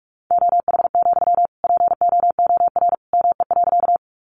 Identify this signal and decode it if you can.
Morse code, heard as a keyed tone, about 44 wpm, 700 Hz. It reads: OH, POOR ME.
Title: Sentences from Top 1000 Words 45wpm